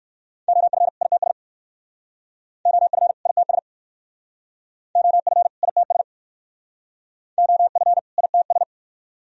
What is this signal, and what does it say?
Morse code, heard as a keyed tone, about 55 wpm, 700 Hz. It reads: OF ITS OF ITS OF ITS OF ITS
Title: Speed-Racing - Sets of 2 Words 45wpm